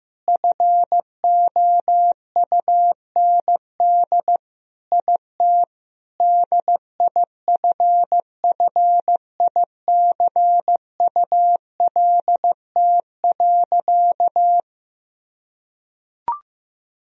Morse code, heard as a keyed tone, about 15 wpm, 700 Hz. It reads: FOUND IT DIFFICULT.
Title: Sentences from Top 1000 Words - Review Part 3 - 15wpm